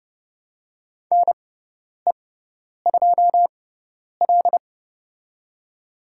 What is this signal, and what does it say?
Morse code, heard as a keyed tone, about 31 wpm, spaced as for 10 wpm, 700 Hz. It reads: NE2L
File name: Lesson 18g - Letter m - Call Signs 30wpm